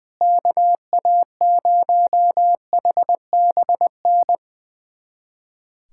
Morse code, heard as a keyed tone, about 20 wpm, 700 Hz. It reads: KA0HBN
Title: USA Callsign DE Callsign - 6 Characters - 20wpm